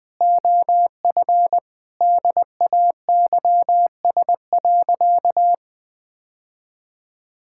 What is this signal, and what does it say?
Morse code, heard as a keyed tone, about 21 wpm, 700 Hz. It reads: OF DAYS.
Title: Sentences with 4-Word Phrases - 20wpm